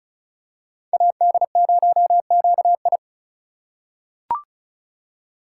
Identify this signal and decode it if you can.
Morse code, heard as a keyed tone, about 36 wpm, 700 Hz. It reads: AD0QI
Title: USA Call Signs - 5 Characters - 4x Character Spacing - 35wpm